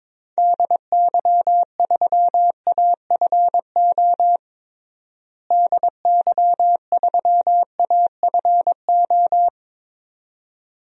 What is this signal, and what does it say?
Morse code, heard as a keyed tone, about 22 wpm, 700 Hz. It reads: DY3AFO DY3AFO
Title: DXCC-100 Call Signs - 6 Characters - Repeated 2x - 22wpm